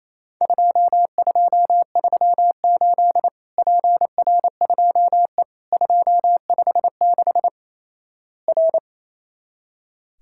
Morse code, heard as a keyed tone, about 28 wpm, 700 Hz. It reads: 2238 PR2E 256 R
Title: Worked All Europe DX Contest - QTC Exchange - 28wpm